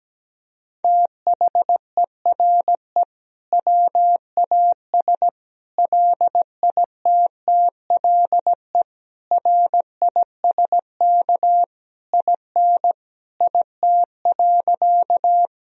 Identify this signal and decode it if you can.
Morse code, heard as a keyed tone, about 17 wpm, 700 Hz. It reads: THERE WAS LITTLE RISK IN IT.
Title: Sentences from Top 1000 Words - Review Part 4 - 17wpm